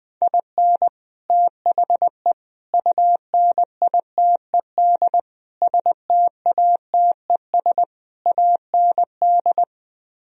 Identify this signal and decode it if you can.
Morse code, heard as a keyed tone, about 20 wpm, 700 Hz. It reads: IN THE UNITED STATES AND